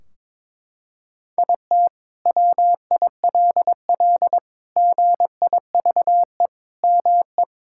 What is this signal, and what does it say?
Morse code, heard as a keyed tone, about 22 wpm, 700 Hz. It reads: IT WILL GIVE ME
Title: Sentences from Top 1600 Words 22wpm